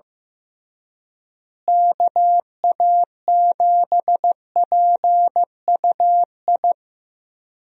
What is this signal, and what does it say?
Morse code, heard as a keyed tone, about 15 wpm, 700 Hz. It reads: KA7PUI
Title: USA Call Signs - 6 Characters - Repeated 3x - 15wpm